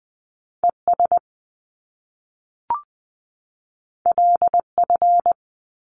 Morse code, heard as a keyed tone, about 20 wpm, 700 Hz. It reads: ES LF
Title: Top Two Letter Combinations in Words 20wpm